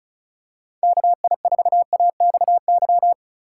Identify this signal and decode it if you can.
Morse code, heard as a keyed tone, about 34 wpm, 700 Hz. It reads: KI4AXY
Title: USA Call Signs - 3x Character Spacing - 35wpm